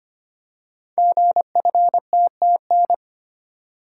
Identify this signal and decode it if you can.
Morse code, heard as a keyed tone, about 25 wpm, 700 Hz. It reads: GFTTN
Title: Code Groups - 5 Letters - 25wpm